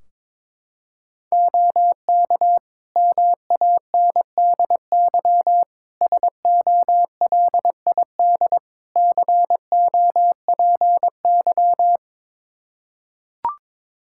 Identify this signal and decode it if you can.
Morse code, heard as a keyed tone, about 22 wpm, 700 Hz. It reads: OK MANDY SOLID COPY
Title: Practice QSO - Exchange 3 - More Info - 22wpm